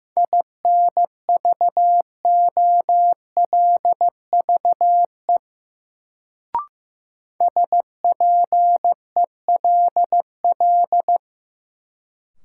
Morse code, heard as a keyed tone, about 15 wpm, 700 Hz. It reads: INVOLVE SPELL
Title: Sentences from Top 1200 Words - New Word Drill Alt - 15wpm